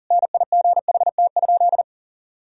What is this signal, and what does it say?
Morse code, heard as a keyed tone, about 40 wpm, 700 Hz. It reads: NIGHT?
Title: Sentences from Top 1000 Words - Review Part 4 - 40wpm